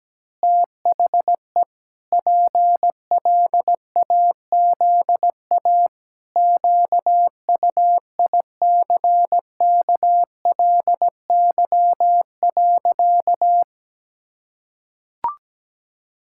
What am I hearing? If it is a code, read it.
Morse code, heard as a keyed tone, about 17 wpm, 700 Hz. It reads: THE PLAZA QUICKLY.
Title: Pangram Challenge v2 - 17wpm